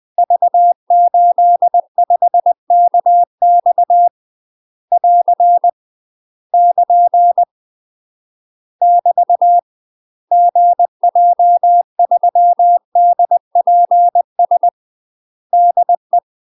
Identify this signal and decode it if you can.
Morse code, heard as a keyed tone, about 20 wpm, 700 Hz. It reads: V85KX <AR> <KN> <BT> GJ3DPS DE